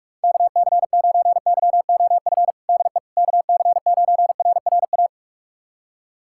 Morse code, heard as a keyed tone, about 47 wpm, 700 Hz. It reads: KC9YOF DE KC9RRA
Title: USA Callsign DE Callsign 45wpm